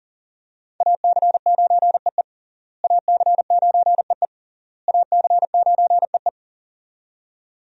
Morse code, heard as a keyed tone, about 37 wpm, 690 Hz. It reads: AC9EE AC9EE AC9EE
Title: USA Call Signs - 5 Characters - Repeated 3x - 40wpm